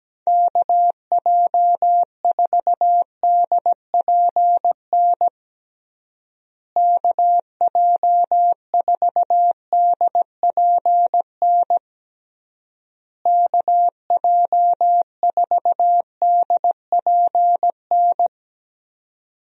Morse code, heard as a keyed tone, about 17 wpm, 700 Hz. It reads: KJ4DPN KJ4DPN KJ4DPN
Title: USA Call Signs - 6 Characters - Repeated 3x - 17wpm